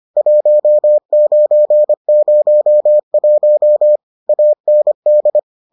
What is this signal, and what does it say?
Morse code, heard as a keyed tone, about 25 wpm, 600 Hz. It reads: 1901 AND